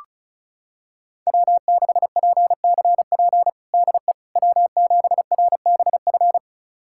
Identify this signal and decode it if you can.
Morse code, heard as a keyed tone, about 36 wpm, 700 Hz. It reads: W6PCP DE W7RBF
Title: USA Callsign DE Callsign - 5 Characters - 35wpm